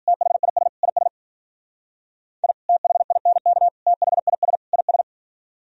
Morse code, heard as a keyed tone, about 48 wpm, 700 Hz. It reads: THIS IS I THINK THIS IS
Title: Speed-Racing - Sets of 4 Words 40wpm